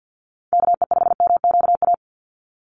Morse code, heard as a keyed tone, about 50 wpm, 700 Hz. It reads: KE5MQA
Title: USA Call Signs 50wpm